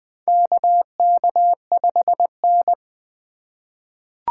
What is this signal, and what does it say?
Morse code, heard as a keyed tone, about 20 wpm, 700 Hz. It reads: KK5N
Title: USA Call Signs - 4x Character Spacing - 20wpm